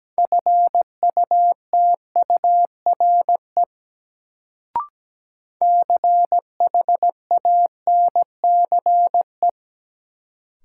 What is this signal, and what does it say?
Morse code, heard as a keyed tone, about 17 wpm, 700 Hz. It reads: FUTURE CHANCE
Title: Sets of 3 Words - New Word Drill - 17wpm